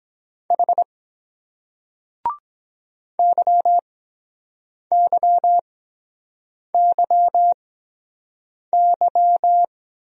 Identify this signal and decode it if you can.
Morse code, heard as a keyed tone, about 23 wpm, spaced as for 7 wpm, 700 Hz. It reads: H YYYY